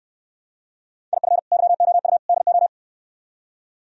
Morse code, heard as a keyed tone, about 67 wpm, 700 Hz. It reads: IF YOU DO